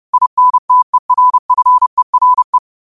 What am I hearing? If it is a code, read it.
Morse code, heard as a keyed tone, about 30 wpm, 1000 Hz. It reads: INTERFERE
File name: qotdfiles / 30wpm